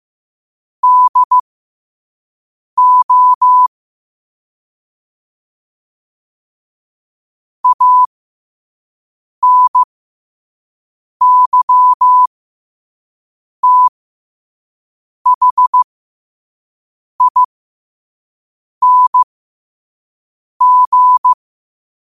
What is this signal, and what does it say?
Morse code, heard as a keyed tone, about 15 wpm, spaced as for 5 wpm, 1000 Hz. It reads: DO ANYTHING
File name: 5wpm